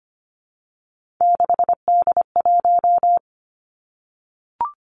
Morse code, heard as a keyed tone, about 25 wpm, 700 Hz. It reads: 6D1
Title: Three Challenging Letter-Number Combinations 25wpm